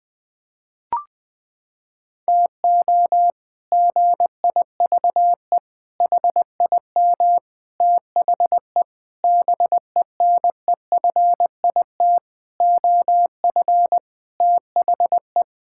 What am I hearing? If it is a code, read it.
Morse code, heard as a keyed tone, about 21 wpm, 700 Hz. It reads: TO GIVE HIM THE BENEFIT OF THE
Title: Sentences with 8-Word Phrases - 20wpm